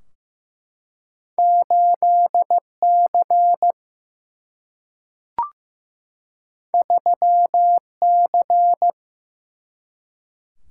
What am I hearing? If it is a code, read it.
Morse code, heard as a keyed tone, about 15 wpm, 700 Hz. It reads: 8C 3C
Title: Single-Letter Single-Number 15wpm